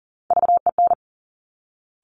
Morse code, heard as a keyed tone, about 40 wpm, 700 Hz. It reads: VEN
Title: Top Three Letter Combinations in Words 40wpm